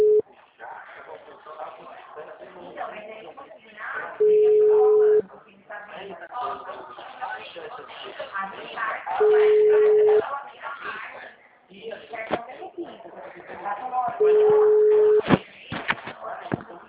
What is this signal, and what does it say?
Ringback tone (Europe, 425 Hz), with speech.